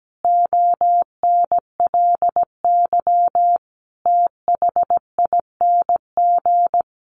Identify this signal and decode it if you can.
Morse code, heard as a keyed tone, about 17 wpm, 700 Hz. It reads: ONLY THING
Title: Sentences from Top 1100 Words 17wpm